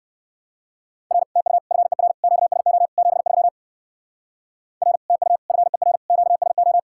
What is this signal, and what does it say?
Morse code, heard as a keyed tone, about 58 wpm, 700 Hz. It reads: R TU FER QSO 73 R TU FER QSO